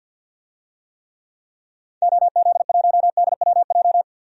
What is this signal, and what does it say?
Morse code, heard as a keyed tone, about 50 wpm, 700 Hz. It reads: OZ1DWJ
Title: DXCC-100 Call Signs - 6 Characters - 50wpm